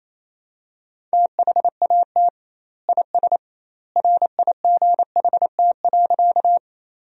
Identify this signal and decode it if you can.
Morse code, heard as a keyed tone, about 28 wpm, 700 Hz. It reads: THAT IS RIGHT.
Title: 2x Word-Spacing - Sentences from Top 200 Words - 28wpm